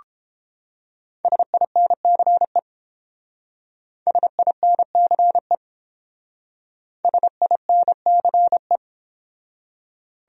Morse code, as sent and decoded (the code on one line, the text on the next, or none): ... .. -. -.-. / ... .. -. -.-. / ... .. -. -.-. .
SINCE SINCE SINCE